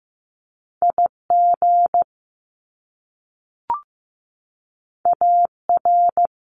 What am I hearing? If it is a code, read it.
Morse code, heard as a keyed tone, about 15 wpm, 700 Hz. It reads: IG AR
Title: Top Two Letter Combinations in Words 15wpm